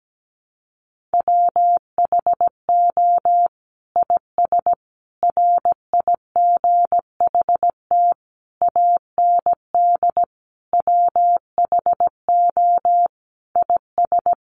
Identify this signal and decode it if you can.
Morse code, heard as a keyed tone, about 17 wpm, 700 Hz. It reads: WHO IS RIGHT AND WHO IS
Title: Sentences from Top 900 Words 17wpm